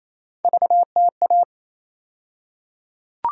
Morse code, heard as a keyed tone, about 27 wpm, 700 Hz. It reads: VTA
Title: USA Call Sign Postfixes - 3 Letters 28wpm